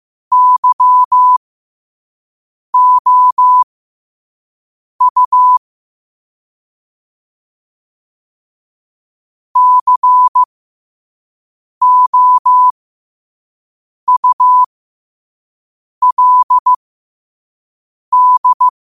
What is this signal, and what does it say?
Morse code, heard as a keyed tone, about 15 wpm, spaced as for 5 wpm, 1000 Hz. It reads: YOU COULD
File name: qotdfiles / 5wpm